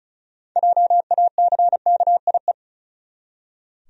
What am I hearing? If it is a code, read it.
Morse code, heard as a keyed tone, about 36 wpm, 700 Hz. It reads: JACKIE
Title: USA Common Names - 35wpm